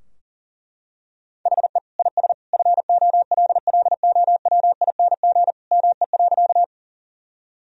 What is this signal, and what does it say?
Morse code, heard as a keyed tone, about 40 wpm, 700 Hz. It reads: HE IS FOLLOWING ME.
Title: Sentences from Top 1000 Words - Review Part 4 - 40wpm